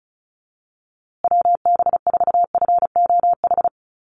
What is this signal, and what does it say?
Morse code, heard as a keyed tone, about 34 wpm, 700 Hz. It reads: WB4FOH
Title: USA Call Signs Repeated 3x 35wpm